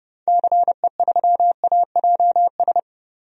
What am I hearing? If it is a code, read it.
Morse code, heard as a keyed tone, about 30 wpm, 700 Hz. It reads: CE3AJS
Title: DXCC-100 Call Signs - 6 Characters - Repeated 3x - 30wpm